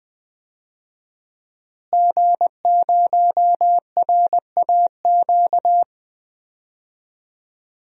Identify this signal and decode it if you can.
Morse code, heard as a keyed tone, about 20 wpm, 700 Hz. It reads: G0RAQ
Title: DXCC-100 Call Signs - 5 Characters - Repeated 2x - 20wpm